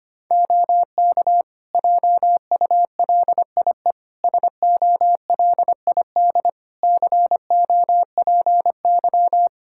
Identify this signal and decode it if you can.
Morse code, heard as a keyed tone, about 25 wpm, 700 Hz. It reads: OK JULIE SOLID COPY
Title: Practice QSO - Exchange 3 - More Info - 25wpm